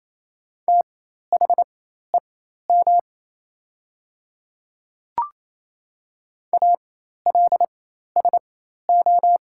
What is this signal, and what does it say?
Morse code, heard as a keyed tone, about 28 wpm, spaced as for 13 wpm, 690 Hz. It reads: THEM ALSO
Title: 4x Character Spacing - Top 300 Words - 28wpm